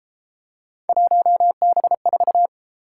Morse code, heard as a keyed tone, about 33 wpm, 700 Hz. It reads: 1B4